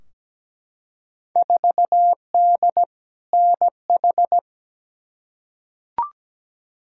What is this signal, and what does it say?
Morse code, heard as a keyed tone, about 17 wpm, 700 Hz. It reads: 4D NH
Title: Field Day Exchanges Encore - 17wpm